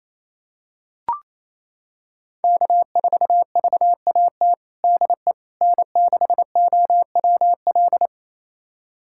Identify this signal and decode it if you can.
Morse code, heard as a keyed tone, about 28 wpm, 700 Hz. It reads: K4VAT DE N6OWL
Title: USA Callsign DE Callsign - 5 Characters - 28wpm